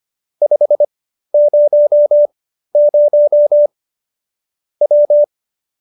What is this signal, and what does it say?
Morse code, heard as a keyed tone, about 25 wpm, spaced as for 13 wpm, 590 Hz. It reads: 500 W